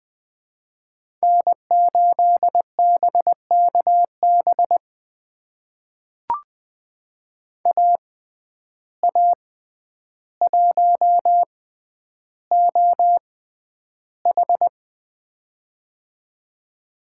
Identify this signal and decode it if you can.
Morse code, heard as a keyed tone, about 20 wpm, spaced as for 10 wpm, 700 Hz. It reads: N8BKB AA1OH
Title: USA Call Signs - 5 Characters - 6x Character Spacing - 20wpm